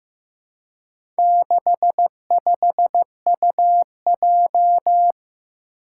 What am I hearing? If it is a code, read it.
Morse code, heard as a keyed tone, about 15 wpm, 700 Hz. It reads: 65UJ
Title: Four Challenging Letter-Number Combinations 15wpm